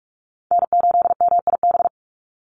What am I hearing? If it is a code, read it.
Morse code, heard as a keyed tone, about 44 wpm, 700 Hz. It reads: N8MIB
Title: USA Call Signs 45wpm